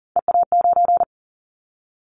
Morse code, heard as a keyed tone, about 40 wpm, 700 Hz. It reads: EA9
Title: Callsign Prefixes for Top 100 DXCC - 40wpm